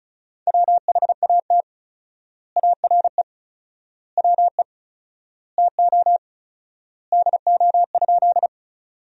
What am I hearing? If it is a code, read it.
Morse code, heard as a keyed tone, about 36 wpm, 700 Hz. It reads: WHAT ARE WE TO DO?
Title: 4x Word-Spacing - Sentences from Top 100 Words - 35wpm